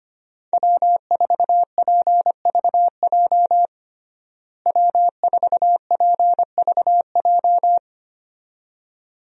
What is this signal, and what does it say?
Morse code, heard as a keyed tone, about 25 wpm, 700 Hz. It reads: W4PVJ W4PVJ
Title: USA Call Signs - 5 Characters - Repeated 2x - 25wpm